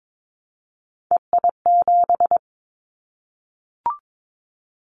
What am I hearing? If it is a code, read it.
Morse code, heard as a keyed tone, about 22 wpm, 700 Hz. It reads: EI7
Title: Callsign Prefixes for Top 50 DXCC - 22wpm